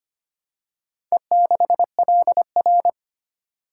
Morse code, heard as a keyed tone, about 25 wpm, 690 Hz. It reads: E6LR
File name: practice / 25wpm / Four Letter-Number Combinations - 25wpm